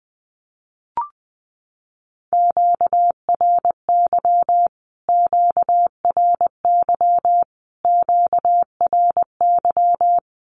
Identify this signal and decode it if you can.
Morse code, heard as a keyed tone, about 20 wpm, 700 Hz. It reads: QRY QRY QRY
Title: Q-Codes Repeated 3x 20wpm